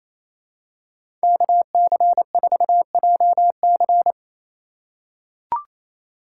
Morse code, heard as a keyed tone, about 27 wpm, 700 Hz. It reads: KC4JC
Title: USA Call Signs - 3x Character Spacing - 28wpm